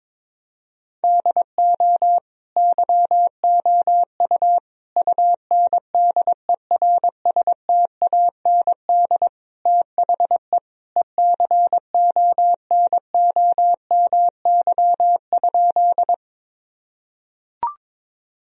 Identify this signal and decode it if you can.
Morse code, heard as a keyed tone, about 22 wpm, 700 Hz. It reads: DO YOU UNDERSTAND THE ECONOMY?
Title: Sentences from Top 1000 Words - Review Part 3 - 22wpm